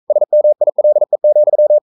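Morse code, heard as a keyed tone, about 42 wpm, 600 Hz. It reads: SMILE,
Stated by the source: Morse code practice at 41 wpm, from CW med Gnister.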